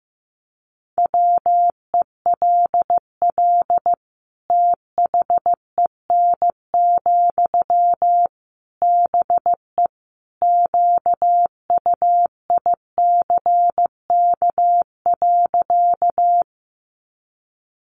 Morse code, heard as a keyed tone, about 15 wpm, 700 Hz. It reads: WELL THEN, BE QUICK.
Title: Sentences from Top 500 Words 15wpm